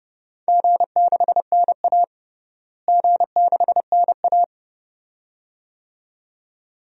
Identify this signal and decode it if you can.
Morse code, heard as a keyed tone, about 30 wpm, 700 Hz. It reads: G6NA G6NA
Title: DXCC-100 Call Signs - 4 Characters - Repeated 2x - 30wpm